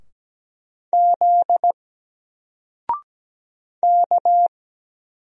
Morse code, heard as a keyed tone, about 17 wpm, 700 Hz. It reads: Z K
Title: Single Character (Letter:Number:Punctuation:Prosign) Alt 17wpm